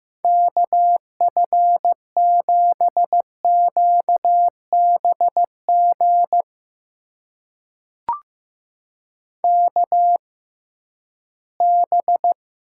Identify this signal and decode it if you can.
Morse code, heard as a keyed tone, about 15 wpm, spaced as for 11 wpm, 700 Hz. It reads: KF7QBG KB
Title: USA Call Signs - 6 Characters - 6x Character Spacing - 15wpm